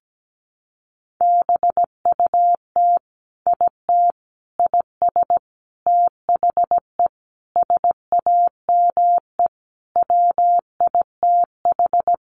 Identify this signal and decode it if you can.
Morse code, heard as a keyed tone, about 17 wpm, 700 Hz. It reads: BUT IT IS THE SAME WITH
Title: Sentences from Top 300 Words 17wpm